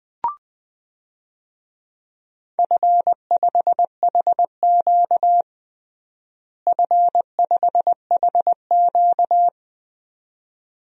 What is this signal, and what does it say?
Morse code, heard as a keyed tone, about 20 wpm, 700 Hz. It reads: F5HQ F5HQ
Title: DXCC-100 Call Signs - 4 Characters - Repeated 2x - 20wpm